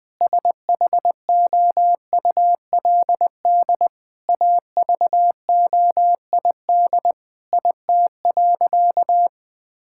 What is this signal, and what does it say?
Morse code, heard as a keyed tone, about 20 wpm, 700 Hz. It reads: SHOULD AVOID IT.
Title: Sentences from Top 1000 Words - Review Part 6 - 20wpm